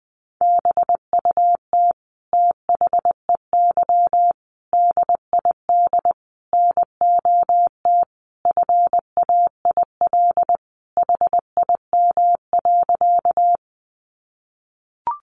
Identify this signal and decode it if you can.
Morse code, heard as a keyed tone, about 19 wpm, 700 Hz. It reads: BUT THEY DID NOT FAIL HIM.
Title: Sentences from Top 1100 Words 20wpm